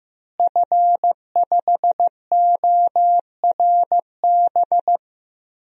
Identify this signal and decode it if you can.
Morse code, heard as a keyed tone, about 15 wpm, 700 Hz. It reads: F5ORB
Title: DXCC-100 Call Signs - 5 Characters - Repeated 2x - 15wpm